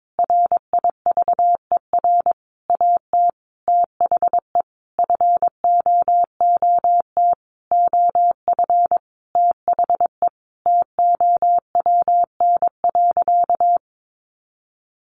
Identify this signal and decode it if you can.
Morse code, heard as a keyed tone, about 22 wpm, 700 Hz. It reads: RIVER AT THE FOOT OF THE TOWN.
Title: Sentences from Top 1400 Words 22wpm